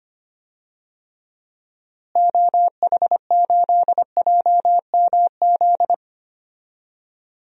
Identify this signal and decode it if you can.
Morse code, heard as a keyed tone, about 25 wpm, 700 Hz. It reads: OH8JMZ
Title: DXCC-100 Call Signs - 6 Characters - Repeated 2x - 25wpm